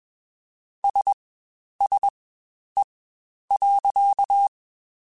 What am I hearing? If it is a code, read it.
Morse code, heard as a keyed tone, about 21 wpm, spaced as for 10 wpm, 800 Hz. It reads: SSE.
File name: Spaced Characters / Spaced Characters-mp3s / Weather 4 14 sp @20